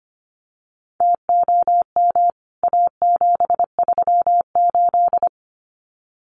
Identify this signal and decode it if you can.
Morse code, heard as a keyed tone, about 25 wpm, 700 Hz. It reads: TOM A738
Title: CWT Exchange 25wpm